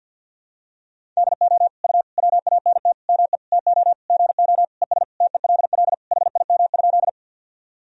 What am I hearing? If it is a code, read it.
Morse code, heard as a keyed tone, about 52 wpm, 700 Hz. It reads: DO U WANT ME TO GO ES TELL HIM?
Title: Sentences from Top 200 Words and Abbreviations 50wpm